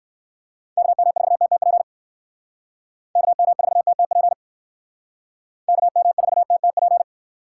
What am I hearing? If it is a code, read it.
Morse code, heard as a keyed tone, about 58 wpm, 700 Hz. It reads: KM4TTP KM4TTP KM4TTP